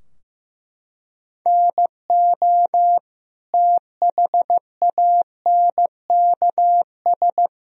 Morse code, heard as a keyed tone, about 15 wpm, 700 Hz. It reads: NO THANKS
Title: Sentences from Top 1000 Words - Review Part 3 - 15wpm